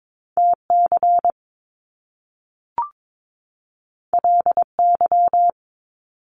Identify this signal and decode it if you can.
Morse code, heard as a keyed tone, about 22 wpm, 700 Hz. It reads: TC LY
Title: Top Two Letter Combinations in Words 22wpm